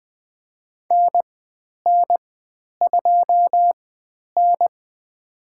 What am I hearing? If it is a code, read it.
Morse code, heard as a keyed tone, about 20 wpm, spaced as for 10 wpm, 700 Hz. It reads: NN2N
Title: 109 Lesson 18g - Letter m - Call Signs 20wpm